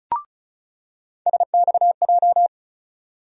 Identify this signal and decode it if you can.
Morse code, heard as a keyed tone, about 36 wpm, 700 Hz. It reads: SXJ
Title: USA Call Sign Postfixes - 3 Letters 35wpm